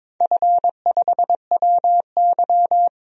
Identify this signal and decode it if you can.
Morse code, heard as a keyed tone, about 22 wpm, 700 Hz. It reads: F5WY
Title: DXCC-100 Call Signs - 4 Characters - 22wpm